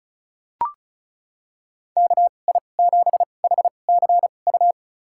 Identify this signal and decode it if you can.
Morse code, heard as a keyed tone, about 36 wpm, spaced as for 25 wpm, 700 Hz. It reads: KI7HCU
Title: USA Call Signs - 6 Characters - 2x Character Spacing - 35wpm